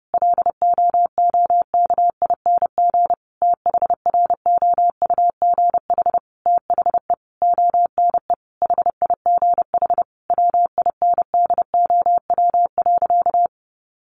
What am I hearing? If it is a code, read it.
Morse code, heard as a keyed tone, about 30 wpm, 700 Hz. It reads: LOOKING THROUGH THE ONE HIGH WINDOW.
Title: A Sound Of Thunder 30wpm